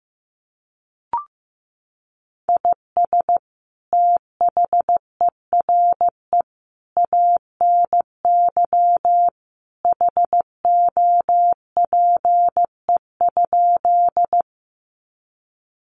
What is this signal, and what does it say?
Morse code, heard as a keyed tone, about 15 wpm, 700 Hz. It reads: IS THERE ANY HOPE?
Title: Sentences from Top 1000 Words 15wpm